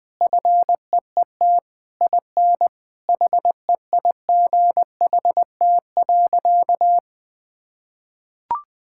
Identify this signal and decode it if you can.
Morse code, heard as a keyed tone, about 20 wpm, 700 Hz. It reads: FEET IN HEIGHT.
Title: Sentences from Top 1600 Words 20wpm